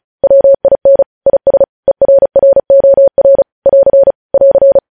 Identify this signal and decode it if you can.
Morse code, heard as a keyed tone, about 36 wpm, 550 Hz. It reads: WIN IS ERROR <AR> <AR>